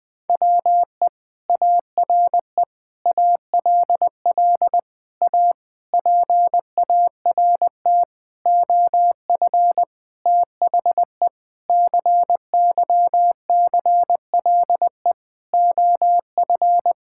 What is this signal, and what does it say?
Morse code, heard as a keyed tone, about 20 wpm, 700 Hz. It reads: WE ARE ALL A PART OF THE CYCLE OF
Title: Sentences from Top 1000 Words - Review Part 6 - 20wpm